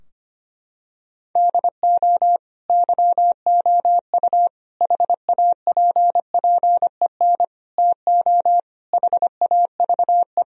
Morse code, heard as a keyed tone, about 25 wpm, 700 Hz. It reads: DO YOU HAPPEN TO HAVE
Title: Sentences from Top 1000 Words - Review Part 2 - 25wpm